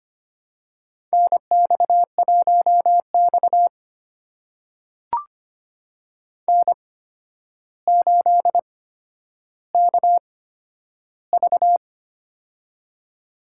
Morse code, heard as a keyed tone, about 25 wpm, spaced as for 11 wpm, 700 Hz. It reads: NX1X N8KV
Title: USA Call Signs - 4 Characters - 8x Character Spacing - 25wpm